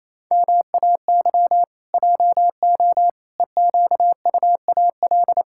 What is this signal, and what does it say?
Morse code, heard as a keyed tone, about 28 wpm, 700 Hz. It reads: MAY JO EQUAL
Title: Pangram Challenge v2 - 28wpm